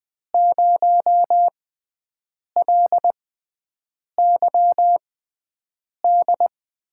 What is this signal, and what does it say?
Morse code, heard as a keyed tone, about 20 wpm, spaced as for 7 wpm, 700 Hz. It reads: 0LYD
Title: USA Call Signs - 6x Character Spacing - 20wpm